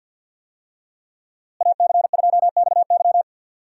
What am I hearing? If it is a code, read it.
Morse code, heard as a keyed tone, about 51 wpm, 700 Hz. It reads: AC2XY
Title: USA Call Signs - 5 Characters - Repeated 2x - 50wpm